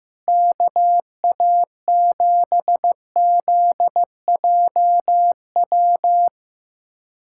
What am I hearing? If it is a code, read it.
Morse code, heard as a keyed tone, about 15 wpm, 700 Hz. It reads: KA7ZJW
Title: USA Callsign DE Callsign - 6 Characters - 15wpm